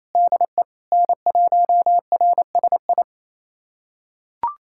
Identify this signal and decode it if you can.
Morse code, heard as a keyed tone, about 28 wpm, 700 Hz. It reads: DE N1RSI
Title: USA Callsign DE Callsign - 5 Characters - 28wpm